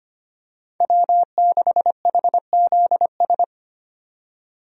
Morse code, heard as a keyed tone, about 25 wpm, 690 Hz. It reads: W6HZS
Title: USA Call Signs - 5 Characters - 25wpm